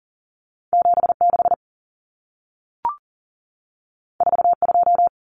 Morse code, heard as a keyed tone, about 40 wpm, 700 Hz. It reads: 76 42
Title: Numbers - 2-Digit 40wpm